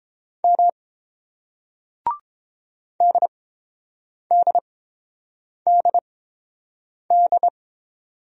Morse code, heard as a keyed tone, about 31 wpm, spaced as for 7 wpm, 700 Hz. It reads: M DDDD